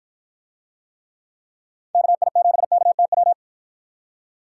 Morse code, heard as a keyed tone, about 53 wpm, 700 Hz. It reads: KI7KTW